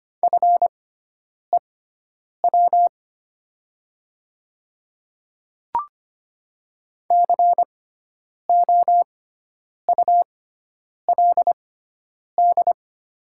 Morse code, heard as a keyed tone, about 25 wpm, spaced as for 9 wpm, 700 Hz. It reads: FEW COULD
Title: 6x Character Spacing - Top 500 Words - 25wpm